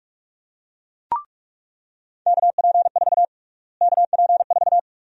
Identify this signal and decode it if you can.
Morse code, heard as a keyed tone, about 47 wpm, 700 Hz. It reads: KP4 KP4
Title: USA Call Sign Prefixes - Repeated 2x - 45wpm